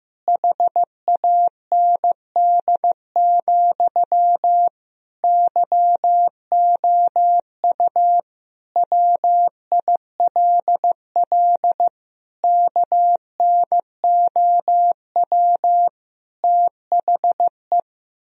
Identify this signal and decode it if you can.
Morse code, heard as a keyed tone, about 15 wpm, 700 Hz. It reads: HAND, YOU WILL KNOW THE
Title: Sentences from Top 1000 Words - Review Part 3 - 15wpm